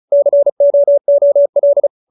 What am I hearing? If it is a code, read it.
Morse code, heard as a keyed tone, about 36 wpm, 570 Hz. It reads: COOL